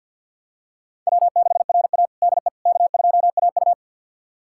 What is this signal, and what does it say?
Morse code, heard as a keyed tone, about 51 wpm, 700 Hz. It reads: W6RA DE K2AU
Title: USA Callsign DE Callsign - 4 Characters - 50wpm